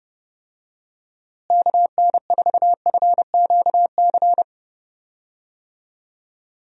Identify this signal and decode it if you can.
Morse code, heard as a keyed tone, about 30 wpm, 700 Hz. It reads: KN4FQC
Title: USA Call Signs - 6 Characters - Repeated 2x - 30wpm